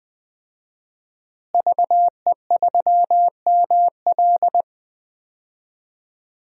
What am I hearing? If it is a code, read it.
Morse code, heard as a keyed tone, about 20 wpm, 700 Hz. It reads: VE3ML
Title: DXCC-100 Call Signs - 5 Characters - Repeated 3x - 20wpm